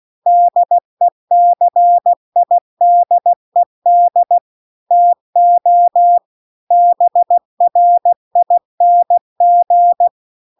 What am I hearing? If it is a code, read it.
Morse code, heard as a keyed tone, about 16 wpm, 700 Hz. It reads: DECIDED TO BRING